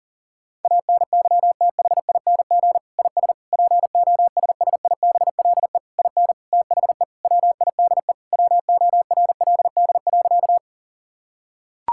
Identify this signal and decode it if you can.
Morse code, heard as a keyed tone, about 40 wpm, 700 Hz. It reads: ANYTHING IS POSSIBLE IN THE WIDE WORLD.
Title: Sentences from Top 1000 Words - Review Part 6 - 40wpm